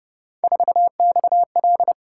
Morse code, heard as a keyed tone, about 29 wpm, 700 Hz. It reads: 4XL